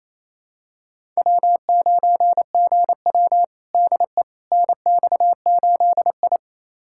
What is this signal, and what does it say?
Morse code, heard as a keyed tone, about 28 wpm, 700 Hz. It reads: W9GW DE NX8I
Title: USA Callsign DE Callsign - 4 Characters - 28wpm